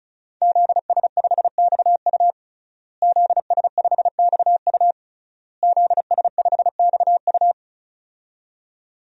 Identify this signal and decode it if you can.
Morse code, heard as a keyed tone, about 36 wpm, 700 Hz. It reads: ZS5XU ZS5XU ZS5XU
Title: DXCC-100 Call Signs - 5 Characters - Repeated 3x - 35wpm